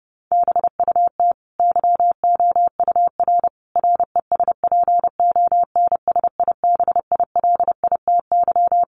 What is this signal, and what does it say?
Morse code, heard as a keyed tone, about 30 wpm, 700 Hz. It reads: BUT YOUR RESPONSIBILITY
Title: Sentences from Top 1200 Words 30wpm